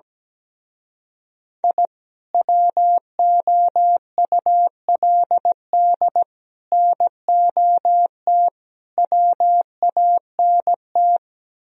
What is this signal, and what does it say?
Morse code, heard as a keyed tone, about 17 wpm, 700 Hz. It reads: I WOULD NOT WANT
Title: Speed-Racing - Sets of 5 Words 17wpm